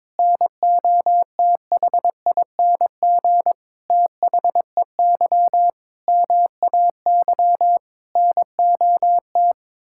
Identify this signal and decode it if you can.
Morse code, heard as a keyed tone, about 22 wpm, 700 Hz. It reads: NOTHING THEY MAY NOT
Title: Sentences from Top 1000 Words - Review Part 6 - 22wpm